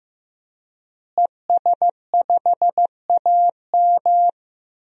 Morse code, heard as a keyed tone, about 15 wpm, 700 Hz. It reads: ES5AM
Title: DXCC-100 Call Signs - 5 Characters - Repeated 3x - 15wpm